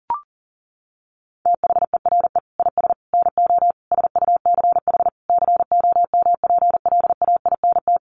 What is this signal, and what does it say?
Morse code, heard as a keyed tone, about 40 wpm, 700 Hz. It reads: THERE IS NO SUCH COMPLAINT
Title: Sentences from Top 1400 Words 40wpm